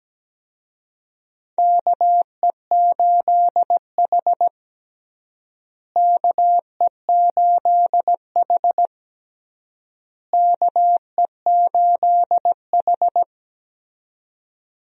Morse code, heard as a keyed tone, about 17 wpm, 700 Hz. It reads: KE8H KE8H KE8H
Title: USA Call Signs - 4 Characters - Repeated 3x - 17wpm